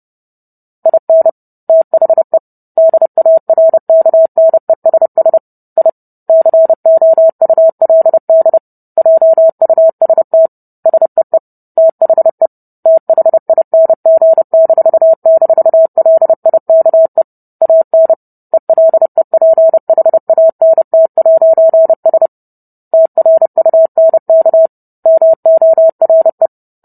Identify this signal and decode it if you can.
Morse code, heard as a keyed tone, about 30 wpm, 650 Hz. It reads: IN THE DARKNESS I COULD JUST SEE THE THING--LIKE AN ELEPHANT'S TRUNK MORE